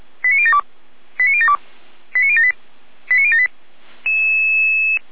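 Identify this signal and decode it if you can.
Five-tone selective call (ZVEI), heard as quick runs of five tones, tones of about 70 ms. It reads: 89072 89072 89078 89078
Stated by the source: ZVEI five-tone alert received by SDR, from feuerwehr-orlamuende.org fire brigade (Orlamünde).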